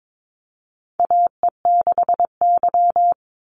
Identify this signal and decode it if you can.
Morse code, heard as a keyed tone, about 22 wpm, 700 Hz. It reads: AE6Y
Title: Call Signs Repeated 2x - 22wpm